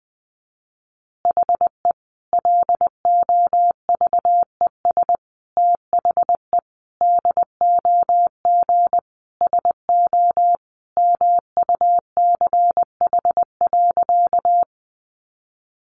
Morse code, heard as a keyed tone, about 20 wpm, 700 Hz. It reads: HE LOVES THE DOG SO MUCH.
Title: Additional Easy Phrases 20wpm